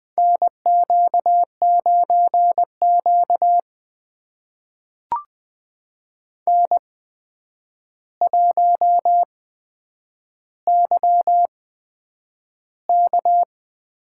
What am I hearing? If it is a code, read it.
Morse code, heard as a keyed tone, about 20 wpm, spaced as for 9 wpm, 700 Hz. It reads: NQ9Q N1YK
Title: USA Call Signs - 4 Characters - 8x Character Spacing - 20wpm